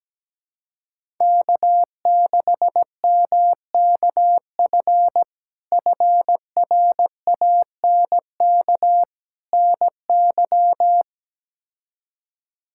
Morse code, heard as a keyed tone, about 17 wpm, 700 Hz. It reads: K6MKF FRANK NY
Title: K1USN SST QSO Practice - 17wpm